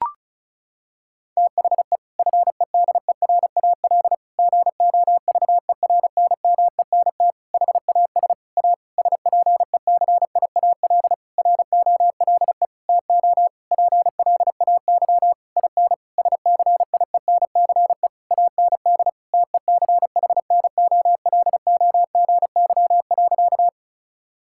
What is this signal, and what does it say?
Morse code, heard as a keyed tone, about 36 wpm, 700 Hz. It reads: THE FEDERAL GOVERNMENT HAS A SPECIAL ROLE TO PLAY IN SCIENCE AND TECHNOLOGY.
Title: Sentences from Top 1000 Words - Review Part 6 - 35wpm